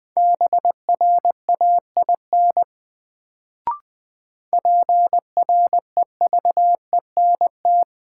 Morse code, heard as a keyed tone, about 21 wpm, 700 Hz. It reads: BRAIN PREVENT
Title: Sentences from Top 1100 Words - New Word Drill Alt - 20wpm